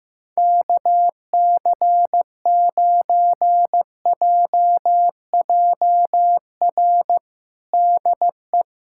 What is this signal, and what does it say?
Morse code, heard as a keyed tone, about 15 wpm, 700 Hz. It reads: KC9JJR DE
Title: USA Callsign DE Callsign - 6 Characters - 15wpm